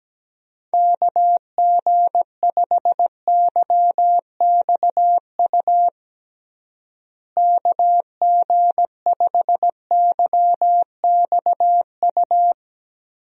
Morse code, heard as a keyed tone, about 17 wpm, 700 Hz. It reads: KG5YXU KG5YXU
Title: USA Call Signs - 6 Characters - Repeated 3x - 17wpm